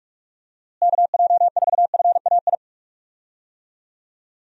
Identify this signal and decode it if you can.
Morse code, heard as a keyed tone, about 46 wpm, 700 Hz. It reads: KJ4FAI
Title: USA Call Signs - 6 Characters - Repeated 3x - 45wpm